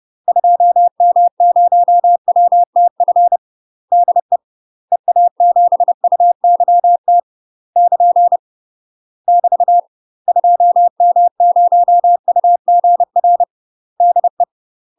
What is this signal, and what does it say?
Morse code, heard as a keyed tone, about 30 wpm, 700 Hz. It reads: 2M0WTF DE EA7UYT <KN> <BT> 2M0UGR DE
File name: UK MASTER QSOs TRAINING/30WPM